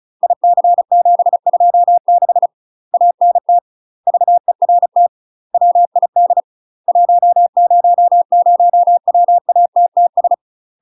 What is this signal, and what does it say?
Morse code, heard as a keyed tone, about 34 wpm, 700 Hz. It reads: IC726 ANT VERT WID 100WATTS
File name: UK MASTER QSOs TRAINING / 35WPM / UK MASTER QSOs 0025